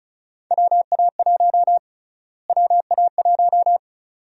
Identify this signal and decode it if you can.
Morse code, heard as a keyed tone, about 36 wpm, 700 Hz. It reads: WA1 WA1
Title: USA Call Sign Prefixes - Repeated 2x - 35wpm